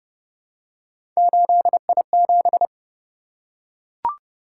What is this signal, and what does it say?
Morse code, heard as a keyed tone, about 26 wpm, 700 Hz. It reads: 8I7